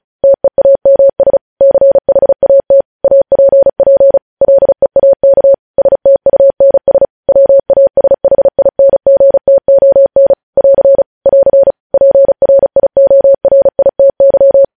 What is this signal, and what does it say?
Morse code, heard as a keyed tone, about 36 wpm, 550 Hz. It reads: TEAMS CHAT APP LEAK STUNS WASHINGTON <AR> <AR> PRIORITY
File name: podcast / News headlines-1x-35x35